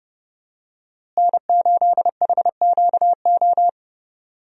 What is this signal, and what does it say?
Morse code, heard as a keyed tone, about 30 wpm, 700 Hz. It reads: N8HQO
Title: USA Call Signs - 5 Characters - Repeated 3x - 30wpm